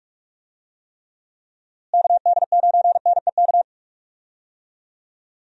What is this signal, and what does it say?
Morse code, heard as a keyed tone, about 46 wpm, 700 Hz. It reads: KD9NEK
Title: USA Call Signs - 6 Characters - Repeated 3x - 45wpm